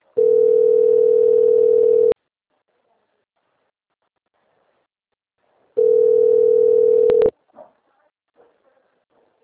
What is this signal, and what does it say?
Ringback tone (North America), 440 + 480 Hz.
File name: out- - - - - .14273